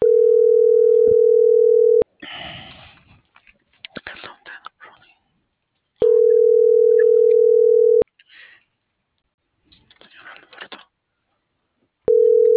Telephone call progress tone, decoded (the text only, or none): ringback tone